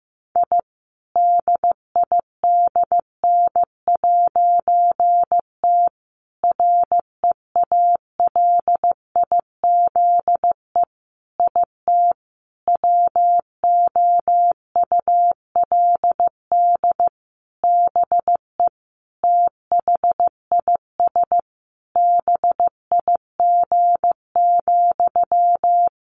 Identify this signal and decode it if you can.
Morse code, heard as a keyed tone, about 15 wpm, 700 Hz. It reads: I DIDN'T REALIZE IT WOULD BE THIS BIG,
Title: A Sound Of Thunder 15wpm